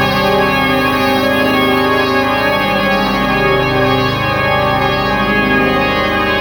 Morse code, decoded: none